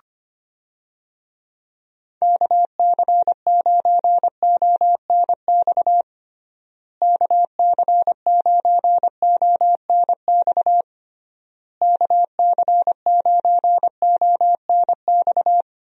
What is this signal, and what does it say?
Morse code, heard as a keyed tone, about 25 wpm, 700 Hz. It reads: KC9ONX KC9ONX KC9ONX
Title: USA Call Signs - 6 Characters - Repeated 3x - 25wpm